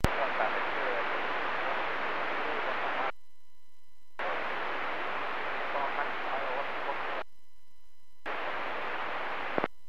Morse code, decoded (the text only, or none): none